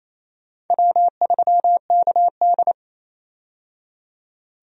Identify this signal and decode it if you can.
Morse code, heard as a keyed tone, about 28 wpm, 700 Hz. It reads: W3KD
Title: USA Call Signs - 4 Characters - 28wpm